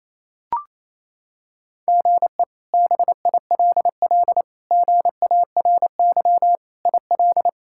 Morse code, heard as a keyed tone, about 28 wpm, 700 Hz. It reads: GE BILL GARY IL
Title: K1USN SST QSO Practice - 28wpm